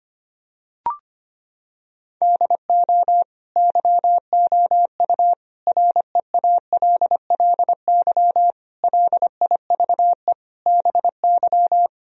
Morse code, heard as a keyed tone, about 25 wpm, 700 Hz. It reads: DO YOU REALLY LIVE BY
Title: Sentences from Top 1000 Words - Review Part 1 - 25wpm